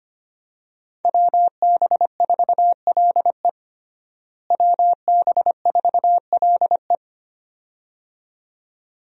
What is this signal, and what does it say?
Morse code, heard as a keyed tone, about 25 wpm, 700 Hz. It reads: WB4LE WB4LE
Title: USA Call Signs - 5 Characters - Repeated 2x - 25wpm